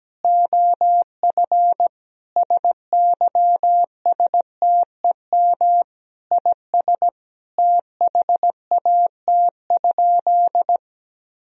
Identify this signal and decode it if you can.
Morse code, heard as a keyed tone, about 17 wpm, 700 Hz. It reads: OF SYSTEM IS THAT?
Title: Sentences from Top 1000 Words - Review Part 3 - 17wpm